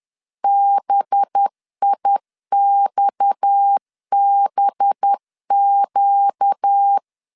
Morse code, heard as a keyed tone, about 10 wpm, 800 Hz. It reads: BIXBQ
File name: LEZIONI / 6 PKX0 / 6 PKX 10wpm 2426